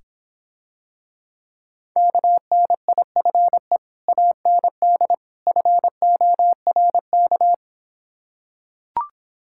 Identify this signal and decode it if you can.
Morse code, heard as a keyed tone, about 26 wpm, 700 Hz. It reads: KNIFE AND FORK